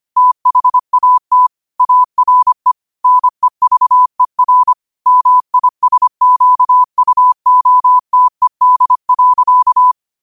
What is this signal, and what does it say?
Morse code, heard as a keyed tone, about 25 wpm, 1000 Hz. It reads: THAT ARE NEVER MISQUOTED.